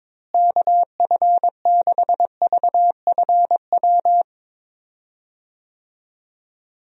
Morse code, heard as a keyed tone, about 21 wpm, 700 Hz. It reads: KF6VFW
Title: USA Call Signs - 6 Characters - Repeated 3x - 22wpm